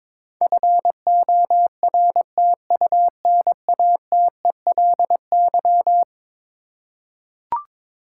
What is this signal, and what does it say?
Morse code, heard as a keyed tone, about 22 wpm, 700 Hz. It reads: FORTUNATELY
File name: practice / 22wpm / Eleven Letter Words - 22wpm